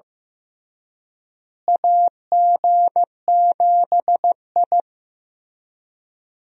Morse code, heard as a keyed tone, about 15 wpm, 700 Hz. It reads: AG7I
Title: USA Call Signs - 4 Characters - Repeated 3x - 15wpm